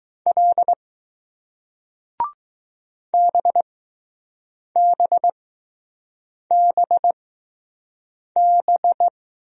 Morse code, heard as a keyed tone, about 20 wpm, spaced as for 6 wpm, 700 Hz. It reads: L BBBB